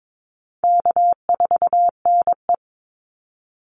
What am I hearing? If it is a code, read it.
Morse code, heard as a keyed tone, about 22 wpm, 700 Hz. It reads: K4NE
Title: Easy Call Signs - 22wpm